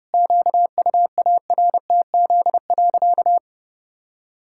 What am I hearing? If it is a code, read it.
Morse code, heard as a keyed tone, about 30 wpm, 700 Hz. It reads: QUARTZ.
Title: Pangram Challenge v2 - 30wpm